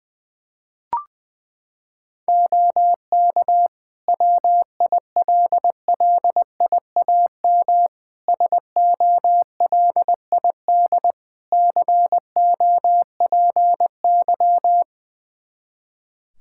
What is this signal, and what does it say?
Morse code, heard as a keyed tone, about 21 wpm, 700 Hz. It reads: OK WILLIAM SOLID COPY
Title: Practice QSO - Exchange 3 - More Info - 20wpm